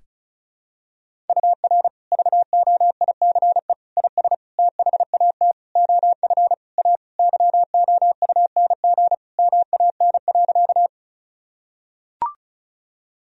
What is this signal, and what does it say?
Morse code, heard as a keyed tone, about 34 wpm, 700 Hz. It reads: UR VOICE IS THAT OF A YOUNG MAN.
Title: Sentences from Top 500 Words and Abbreviations 35wpm